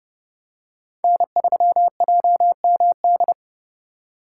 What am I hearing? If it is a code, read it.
Morse code, heard as a keyed tone, about 30 wpm, 700 Hz. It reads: N3JMD
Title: USA Call Signs - 4x Character Spacing - 30wpm